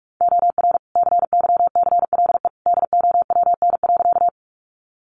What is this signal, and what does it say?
Morse code, heard as a keyed tone, about 44 wpm, 700 Hz. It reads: OR CYCLE DOWN.
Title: Sentences from Top 900 Words 45wpm